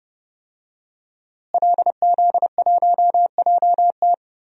Morse code, heard as a keyed tone, about 30 wpm, 700 Hz. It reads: LZ1JT
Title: DXCC-100 Call Signs - 5 Characters - Repeated 3x - 30wpm